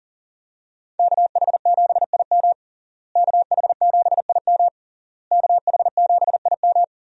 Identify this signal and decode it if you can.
Morse code, heard as a keyed tone, about 40 wpm, 700 Hz. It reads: KH7IM KH7IM KH7IM
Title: USA Call Signs - 5 Characters - Repeated 3x - 40wpm